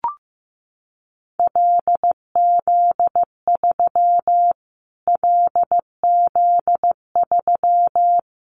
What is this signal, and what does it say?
Morse code, heard as a keyed tone, about 15 wpm, 700 Hz. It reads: LZ3 LZ3
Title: Callsign Prefixes for Top 25 DXCC - Repeated 3x - 15wpm